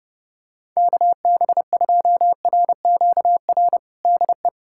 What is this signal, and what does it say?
Morse code, heard as a keyed tone, about 32 wpm, 700 Hz. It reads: KB2RQR DE
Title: USA Callsign DE Callsign - 6 Characters - 30wpm